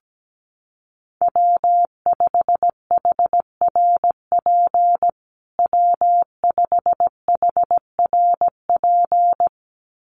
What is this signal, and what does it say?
Morse code, heard as a keyed tone, about 17 wpm, 700 Hz. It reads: W5HRP W5HRP
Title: USA Call Signs Repeated 2x 17wpm